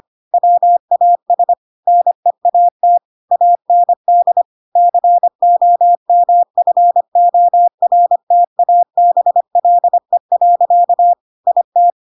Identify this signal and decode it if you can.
Morse code, heard as a keyed tone, about 25 wpm, 700 Hz. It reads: WAS NEAT AND COMFORTABLE. IT
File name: Pinocchio / Pinocchio ch3 25 25 ik1hge